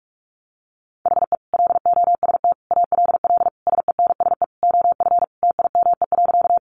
Morse code, heard as a keyed tone, about 44 wpm, 700 Hz. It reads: HE LOST ALL SENSE OF TIME.
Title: Sentences from Top 1400 Words 45wpm